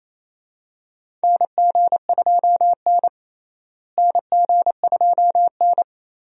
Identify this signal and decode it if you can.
Morse code, heard as a keyed tone, about 27 wpm, 700 Hz. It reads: NG2N NG2N
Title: USA Call Signs - 4 Characters - Repeated 2x - 28wpm